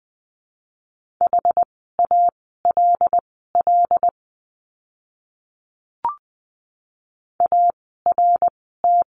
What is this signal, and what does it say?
Morse code, heard as a keyed tone, about 19 wpm, spaced as for 14 wpm, 700 Hz. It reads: HALL ART
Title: 075 Lesson 13f - Letter L - Words 20wpm